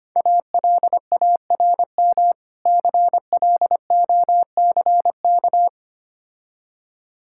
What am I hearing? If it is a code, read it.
Morse code, heard as a keyed tone, about 25 wpm, 700 Hz. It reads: ALARM CLOCK
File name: Easy Words - Speed-Racing with Hints - 25wpm